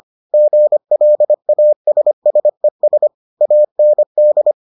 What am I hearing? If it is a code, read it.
Morse code, heard as a keyed tone, about 25 wpm, 600 Hz. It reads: GLASSES AND